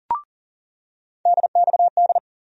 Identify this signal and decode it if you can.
Morse code, heard as a keyed tone, about 37 wpm, 700 Hz. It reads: DXD
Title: USA Call Sign Postfixes - 3 Letters 40wpm